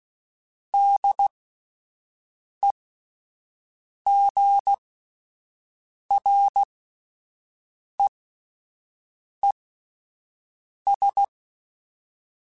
Morse code, heard as a keyed tone, about 16 wpm, spaced as for 5 wpm, 800 Hz. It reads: DEGREES
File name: Spaced Characters / Spaced Characters-mp3s / Weather 4 14 sp @10